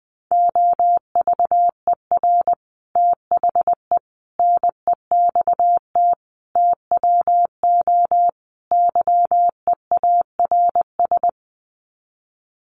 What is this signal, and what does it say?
Morse code, heard as a keyed tone, about 20 wpm, 700 Hz. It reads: OVER THE NEXT TWO YEARS
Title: Sets of 5 Words Encore2 20wpm